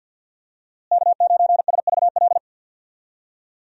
Morse code, heard as a keyed tone, about 50 wpm, 700 Hz. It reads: K9SVL
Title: USA Call Signs - 5 Characters - Repeated 3x - 50wpm